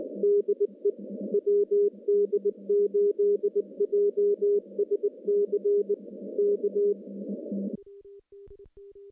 Morse code, heard as a keyed tone, about 20 wpm, 410 Hz. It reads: DE WD8JSC K
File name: contests / CW-SS-k5tr- -0700